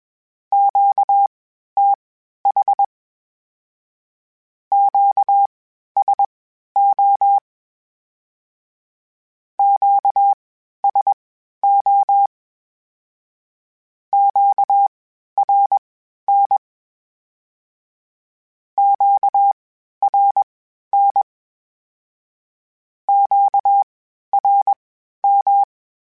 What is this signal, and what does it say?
Morse code, heard as a keyed tone, about 21 wpm, spaced as for 12 wpm, 800 Hz. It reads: QTH QSO QSO QRN QRN QRM
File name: (spaced) / Q-Calls Spaced-mp3s / Q Calls Twice sp @20